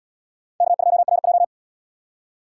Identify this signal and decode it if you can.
Morse code, heard as a keyed tone, about 64 wpm, 700 Hz. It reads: K2RJ